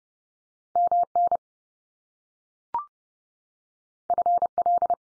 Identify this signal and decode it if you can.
Morse code, heard as a keyed tone, about 30 wpm, 700 Hz. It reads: MN FL
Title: US States 30WPM